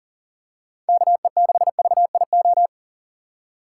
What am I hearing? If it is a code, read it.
Morse code, heard as a keyed tone, about 38 wpm, 700 Hz. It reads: KE6VIO
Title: USA Call Signs - 6 Characters - 4x Character Spacing - 40wpm